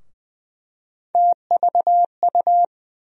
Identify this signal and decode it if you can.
Morse code, heard as a keyed tone, about 20 wpm, 700 Hz. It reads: TVU
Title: USA Call Sign Postfixes - 3 Letters 20wpm